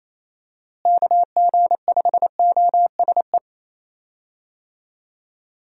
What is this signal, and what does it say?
Morse code, heard as a keyed tone, about 28 wpm, 700 Hz. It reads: KG5OSE
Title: USA Call Signs - 6 Characters - Repeated 2x - 28wpm